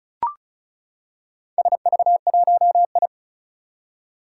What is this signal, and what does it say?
Morse code, heard as a keyed tone, about 36 wpm, 690 Hz. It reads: SV1I
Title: Four Challenging Letter-Number Combinations 35wpm